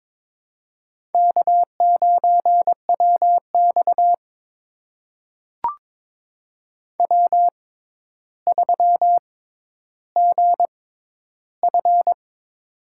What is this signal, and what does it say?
Morse code, heard as a keyed tone, about 21 wpm, spaced as for 11 wpm, 700 Hz. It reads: K9WX W3GF
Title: USA Call Signs - 4 Characters - 6x Character Spacing - 22wpm